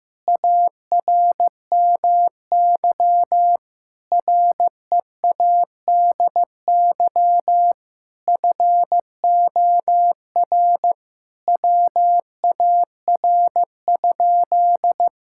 Morse code, heard as a keyed tone, about 15 wpm, 700 Hz. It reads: ARMY READY FOR WAR?
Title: Sentences from Top 1000 Words - Review Part 5 - 15wpm